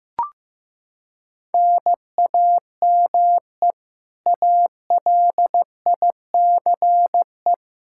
Morse code, heard as a keyed tone, about 15 wpm, 700 Hz. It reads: NAME ALICE
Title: Practice QSO - Exchange 1 - Beginning - 15wpm